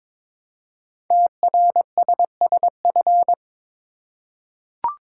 Morse code, heard as a keyed tone, about 22 wpm, 700 Hz. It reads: TRSSF
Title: Code Groups - 5 Letters - 22wpm